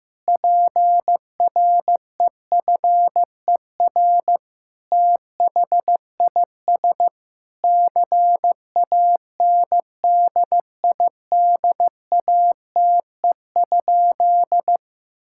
Morse code, heard as a keyed tone, about 15 wpm, 700 Hz. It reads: PREFER THIS CANDIDATE?
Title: Sentences from Top 1600 Words 15wpm